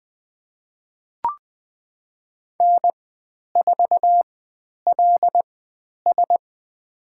Morse code, heard as a keyed tone, about 20 wpm, spaced as for 10 wpm, 700 Hz. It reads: N4LS
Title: 095 Lesson 16g - Letter c - Call Signs 20wpm